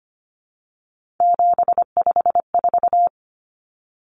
Morse code, heard as a keyed tone, about 25 wpm, 700 Hz. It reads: 754
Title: Common Cut Numbers - 3-Digit 25wpm